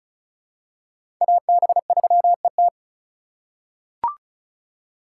Morse code, heard as a keyed tone, about 36 wpm, 700 Hz. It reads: AB3ET
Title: USA Call Signs - 5 Characters - 2x Character Spacing - 35wpm